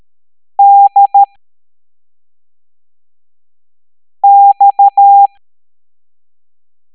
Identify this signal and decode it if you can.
Morse code, heard as a keyed tone, about 13 wpm, spaced as for 3 wpm, 800 Hz. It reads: DX